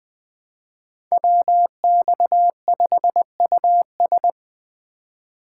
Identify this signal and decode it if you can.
Morse code, heard as a keyed tone, about 20 wpm, 700 Hz. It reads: WX5US
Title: Speed Racing - USA 2x2 Call Signs 17wpm